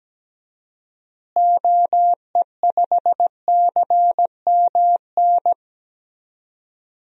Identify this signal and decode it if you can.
Morse code, heard as a keyed tone, about 17 wpm, 700 Hz. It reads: OE5CMN
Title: DXCC-100 Call Signs - 6 Characters - Repeated 2x - 17wpm